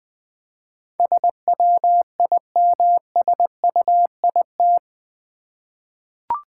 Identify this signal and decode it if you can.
Morse code, heard as a keyed tone, about 21 wpm, 700 Hz. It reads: SWIMSUIT
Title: Easy Words with Hints - 20wpm